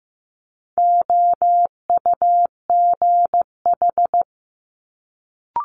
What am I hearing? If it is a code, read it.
Morse code, heard as a keyed tone, about 15 wpm, 700 Hz. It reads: OUGH
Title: Multi-letter Phonemes Quick Fire 15wpm